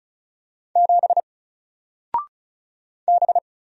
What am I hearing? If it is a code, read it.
Morse code, heard as a keyed tone, about 36 wpm, 700 Hz. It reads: 7 B
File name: Single-Letter-Number-35wpm